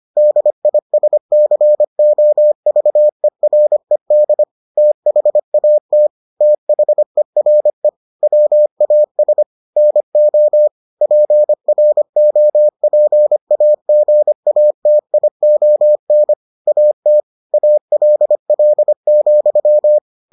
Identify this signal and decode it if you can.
Morse code, heard as a keyed tone, about 25 wpm, 600 Hz. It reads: DISCOVERED THAT THERE WAS NO PROPAGATION AT ALL,